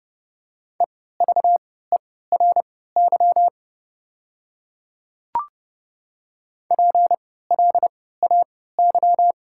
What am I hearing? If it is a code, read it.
Morse code, heard as a keyed tone, about 29 wpm, spaced as for 17 wpm, 690 Hz. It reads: EVERY PLAY
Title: 3x Character Spacing - Top 200 Words - 30wpm